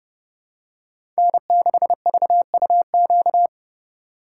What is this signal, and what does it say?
Morse code, heard as a keyed tone, about 30 wpm, 700 Hz. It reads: N6VUQ
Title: USA Call Signs - 5 Characters - 4x Character Spacing - 30wpm